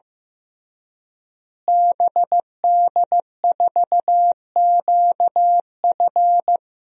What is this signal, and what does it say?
Morse code, heard as a keyed tone, about 15 wpm, 700 Hz. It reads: BD4QF
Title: DXCC-100 Call Signs - 5 Characters - Repeated 3x - 15wpm